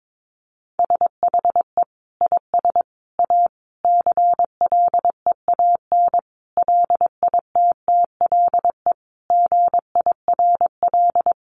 Morse code, heard as a keyed tone, about 22 wpm, 700 Hz. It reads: SHE IS A CLEAN LITTLE GIRL
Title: Sentences from Top 1100 Words 22wpm